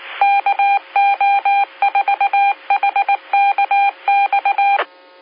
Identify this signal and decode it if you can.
Morse code, heard as a keyed tone, about 19 wpm, 800 Hz. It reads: KO4HKX
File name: SimplexLogic